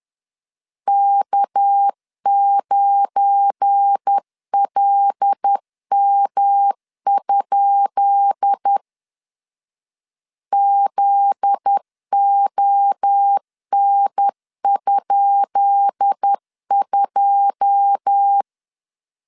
Morse code, heard as a keyed tone, about 10 wpm, 800 Hz. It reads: K9LM? ZON?2
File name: LEZIONI / 10 SIGNS / 10B INT 10wpm 2844